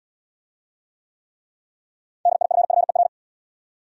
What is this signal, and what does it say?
Morse code, heard as a keyed tone, about 68 wpm, 700 Hz. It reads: DIFFER